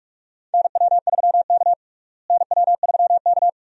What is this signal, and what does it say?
Morse code, heard as a keyed tone, about 46 wpm, 700 Hz. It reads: NW3K NW3K
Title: USA Call Signs - 4 Characters - Repeated 2x - 45wpm